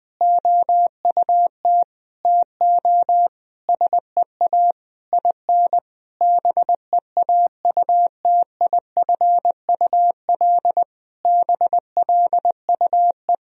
Morse code, heard as a keyed tone, about 20 wpm, 700 Hz. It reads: OUT TO SEA IN BEAUTIFUL BLUE
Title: Sentences from Top 1000 Words - Review Part 3 - 20wpm